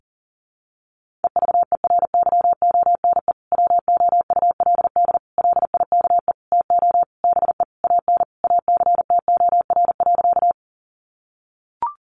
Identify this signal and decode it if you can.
Morse code, heard as a keyed tone, about 40 wpm, 700 Hz. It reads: EVERYONE WOULD LIKE TO BE AN ACTOR.
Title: Sentences from Top 1500 Words 40wpm